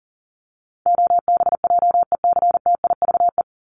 Morse code, heard as a keyed tone, about 40 wpm, 700 Hz. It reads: OBJECTIVE
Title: Top 1001-1250 Words 40wpm